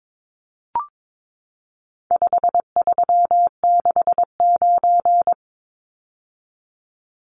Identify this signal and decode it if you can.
Morse code, heard as a keyed tone, about 22 wpm, 700 Hz. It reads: 5369
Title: Numbers - 4-Digit 22wpm